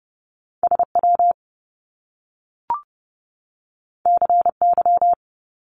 Morse code, heard as a keyed tone, about 30 wpm, 700 Hz. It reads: SW CY
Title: Top Two Letter Combinations in Words 30wpm